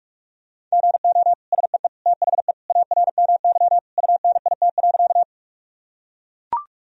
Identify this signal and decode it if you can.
Morse code, heard as a keyed tone, about 42 wpm, 700 Hz. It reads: GO SEE THE ARMY UNIT.
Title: Sentences from Top 1000 Words - Review Part 2 - 45wpm